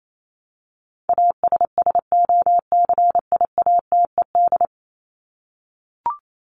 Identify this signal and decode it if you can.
Morse code, heard as a keyed tone, about 28 wpm, 700 Hz. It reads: ASSOCIATED
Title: ED Suffix - Top 100 Words with Base Words 28wpm